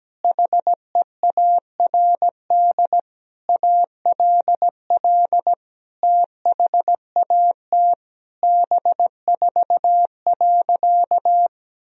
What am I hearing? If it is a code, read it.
Morse code, heard as a keyed tone, about 17 wpm, 700 Hz. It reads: HEARD ALL THAT B4.
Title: Sentences from Top 400 Words and Abbreviations 17wpm